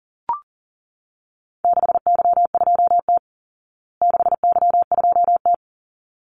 Morse code, heard as a keyed tone, about 37 wpm, 700 Hz. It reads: 6Y2T 6Y2T
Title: Call Signs Repeated 3x - 40wpm